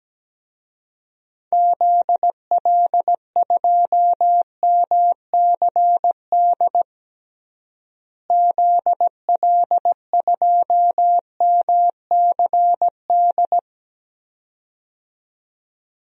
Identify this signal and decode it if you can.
Morse code, heard as a keyed tone, about 17 wpm, 700 Hz. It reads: ZL2MCD ZL2MCD
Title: DXCC-100 Call Signs - 6 Characters - Repeated 2x - 17wpm